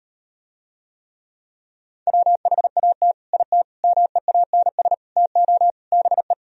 Morse code, heard as a keyed tone, about 38 wpm, 700 Hz. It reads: WHAT IT MEANS TO BE